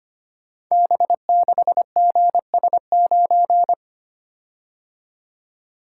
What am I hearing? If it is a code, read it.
Morse code, heard as a keyed tone, about 25 wpm, 700 Hz. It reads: B6GS9
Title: Code Groups - 5 Letter:Numbers - 25wpm